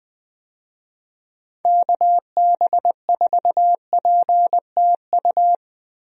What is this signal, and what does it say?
Morse code, heard as a keyed tone, about 20 wpm, 700 Hz. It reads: KB4PTU
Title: USA Call Signs - 6 Characters - Repeated 2x - 20wpm